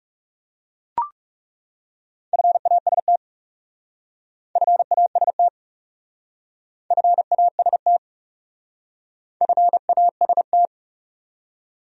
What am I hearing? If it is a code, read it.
Morse code, heard as a keyed tone, about 38 wpm, 700 Hz. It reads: FAST FAST FAST FAST